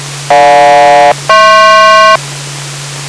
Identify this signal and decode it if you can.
Aircraft SELCAL, heard as two pairs of tones. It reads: GJ-HP